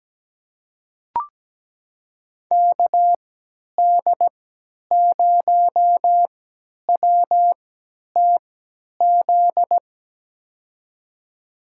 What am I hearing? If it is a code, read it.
Morse code, heard as a keyed tone, about 17 wpm, spaced as for 10 wpm, 700 Hz. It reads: KD0WTZ
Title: USA Call Signs - 6 Characters - 3x Character Spacing - 17wpm